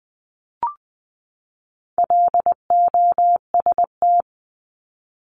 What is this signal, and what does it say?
Morse code, heard as a keyed tone, about 20 wpm, 700 Hz. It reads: LOST
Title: Top 1251-1500 Words 20wpm